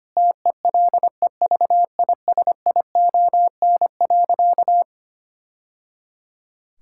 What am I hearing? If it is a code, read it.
Morse code, heard as a keyed tone, about 25 wpm, 700 Hz. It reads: TELEVISION.
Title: Sentences from Top 1000 Words - Review Part 5 - 25wpm